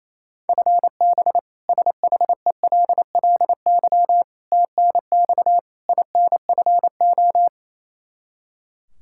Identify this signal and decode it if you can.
Morse code, heard as a keyed tone, about 28 wpm, 700 Hz. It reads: FB SHELLY TNX INFO
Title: K1USN SST QSO Practice 28wpm